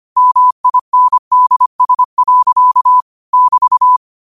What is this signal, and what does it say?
Morse code, heard as a keyed tone, about 25 wpm, 1000 Hz. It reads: MINDS. <BT>
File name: qotdfiles / 25wpm